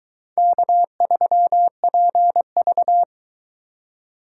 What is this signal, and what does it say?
Morse code, heard as a keyed tone, about 23 wpm, 700 Hz. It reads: K3PV